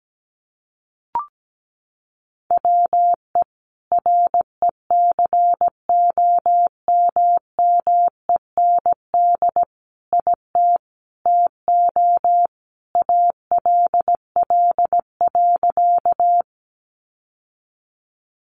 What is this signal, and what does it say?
Morse code, heard as a keyed tone, about 17 wpm, 700 Hz. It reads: WE RECOMMEND IT TO ALL.
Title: Sentences from Top 1200 Words 17wpm